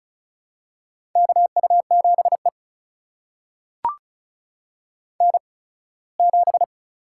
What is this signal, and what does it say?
Morse code, heard as a keyed tone, about 36 wpm, spaced as for 21 wpm, 700 Hz. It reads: KU7E N7
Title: USA Call Signs - 4 Characters - 8x Character Spacing - 35wpm